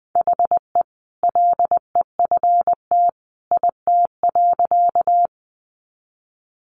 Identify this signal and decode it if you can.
Morse code, heard as a keyed tone, about 20 wpm, 700 Hz. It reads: HE LEFT IT.
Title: A Sound Of Thunder 20wpm